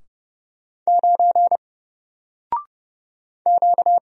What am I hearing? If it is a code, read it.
Morse code, heard as a keyed tone, about 30 wpm, 700 Hz. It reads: 9 Q
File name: Single-Letter-Number-30wpm